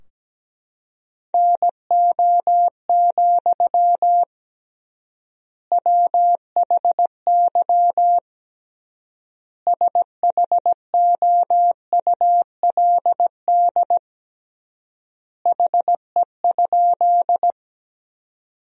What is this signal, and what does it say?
Morse code, heard as a keyed tone, about 17 wpm, 700 Hz. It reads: NO, WHY SHOULD HE?
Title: 3x Word-Spacing - Sentences from Top 300 Words - 17wpm